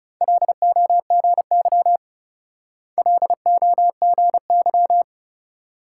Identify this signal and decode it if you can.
Morse code, heard as a keyed tone, about 36 wpm, 700 Hz. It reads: LOGY LOGY
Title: Speed-Racing Multi-letter Phonemes - 30wpm